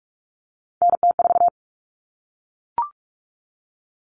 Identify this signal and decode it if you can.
Morse code, heard as a keyed tone, about 44 wpm, 700 Hz. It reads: NT4
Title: Common Cut Numbers - 3-Digit 45wpm